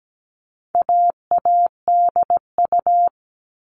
Morse code, heard as a keyed tone, about 17 wpm, 700 Hz. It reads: AADU
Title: All Cut Numbers - 4-Digit 17wpm